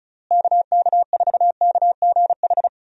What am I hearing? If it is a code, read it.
Morse code, heard as a keyed tone, about 36 wpm, 700 Hz. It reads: KK4KGH
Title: USA Call Signs - 6 Characters - Repeated 2x - 35wpm